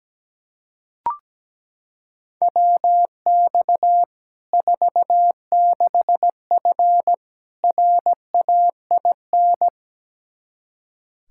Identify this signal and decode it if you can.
Morse code, heard as a keyed tone, about 17 wpm, 700 Hz. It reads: WX 46F RAIN
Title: Practice QSO - Exchange 2 - Info - 17wpm